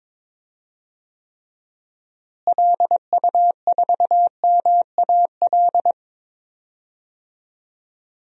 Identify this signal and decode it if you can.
Morse code, heard as a keyed tone, about 22 wpm, 700 Hz. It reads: LU4MAL
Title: DXCC-100 Call Signs - 6 Characters - Repeated 3x - 22wpm